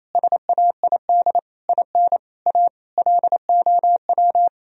Morse code, heard as a keyed tone, about 28 wpm, 700 Hz. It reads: SAID IN A LOW
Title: Sentences from Top 1000 Words - Review Part 1 - 28wpm